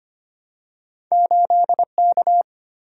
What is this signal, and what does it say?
Morse code, heard as a keyed tone, about 25 wpm, 700 Hz. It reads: 8K
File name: Two Letter-Number Combinations - 25wpm